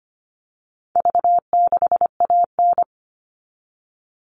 Morse code, heard as a keyed tone, about 25 wpm, 700 Hz. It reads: V6AN
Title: All Cut Numbers - 4-Digit 25wpm